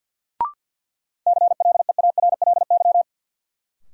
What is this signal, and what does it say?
Morse code, heard as a keyed tone, about 53 wpm, 700 Hz. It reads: CLEARLY
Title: Sentences from Top 1300 Words - New Word Drill Alt - 50wpm